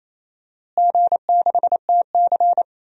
Morse code, heard as a keyed tone, about 27 wpm, 700 Hz. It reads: G6TC
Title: DXCC-100 Call Signs - 4 Characters - Repeated 2x - 28wpm